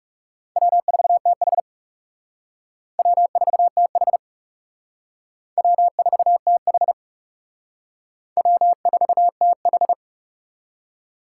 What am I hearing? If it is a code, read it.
Morse code, heard as a keyed tone, about 35 wpm, 700 Hz. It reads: W4TH W4TH W4TH W4TH
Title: Speed Racing - USA 1x2 Call Signs 30wpm